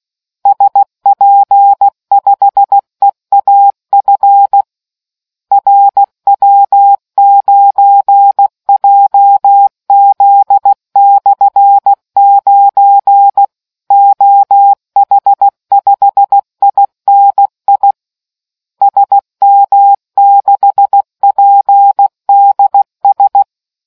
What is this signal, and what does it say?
Morse code, heard as a keyed tone, about 16 wpm, 800 Hz. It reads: SP5EAF RW9JZ/9 OH5INI SM6PDS